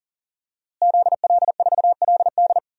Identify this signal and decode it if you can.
Morse code, heard as a keyed tone, about 40 wpm, 700 Hz. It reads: ZL4LD
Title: DXCC-100 Call Signs - 5 Characters - Repeated 3x - 40wpm